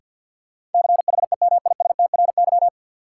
Morse code, heard as a keyed tone, about 53 wpm, 700 Hz. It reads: CHEMISTRY
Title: Sentences from Top 1000 Words - Review Part 6 - New Word Drill - 50wpm